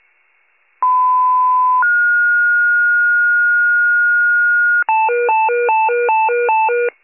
Two-tone page, heard as a one-second tone, then a three-second tone, about 1010 then 1430 Hz. It is heard from 0.8 to 4.8 s.